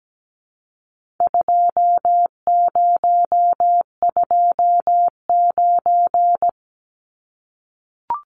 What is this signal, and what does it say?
Morse code, heard as a keyed tone, about 17 wpm, 700 Hz. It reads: 2029
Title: Numbers - 4-Digit 17wpm